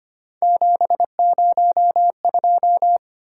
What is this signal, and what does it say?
Morse code, heard as a keyed tone, about 25 wpm, 700 Hz. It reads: 702
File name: Three Letter-Number Combinations - 25wpm